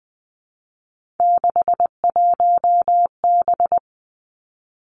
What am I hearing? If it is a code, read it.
Morse code, heard as a keyed tone, about 20 wpm, 700 Hz. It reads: 61B
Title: Three Challenging Letter-Number Combinations 20wpm